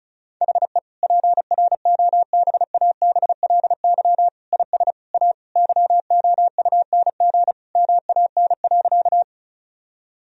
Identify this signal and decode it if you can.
Morse code, heard as a keyed tone, about 34 wpm, 700 Hz. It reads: HE PROBABLY IS A YOUNG MAN.
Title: Sentences from Top 1000 Words - Review Part 4 - 35wpm